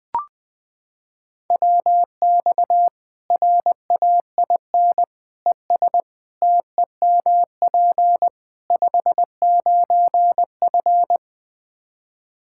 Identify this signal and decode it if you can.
Morse code, heard as a keyed tone, about 20 wpm, 700 Hz. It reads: WX RAIN ES TEMP 59F
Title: Practice QSO - Exchange 2 - Info - 20wpm